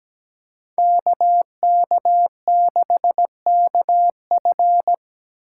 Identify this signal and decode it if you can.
Morse code, heard as a keyed tone, about 17 wpm, 700 Hz. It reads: KK6KF
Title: USA Call Signs - 5 Characters - 8x Character Spacing - 17wpm